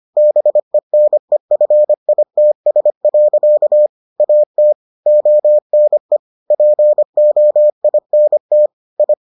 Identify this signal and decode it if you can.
Morse code, heard as a keyed tone, about 25 wpm, 600 Hz. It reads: BENEFITS. AT ONE POINT I